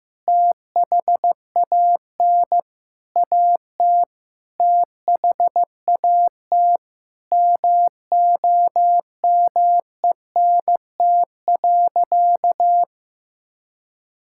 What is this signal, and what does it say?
Morse code, heard as a keyed tone, about 15 wpm, 700 Hz. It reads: THAN AT THAT MOMENT.
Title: Sentences from Top 1000 Words - Review Part 6 - 15wpm